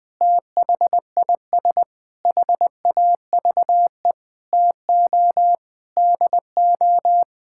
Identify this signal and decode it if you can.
Morse code, heard as a keyed tone, about 21 wpm, 700 Hz. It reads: THIS HAVE TO DO
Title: Sets of 9 Words - 20wpm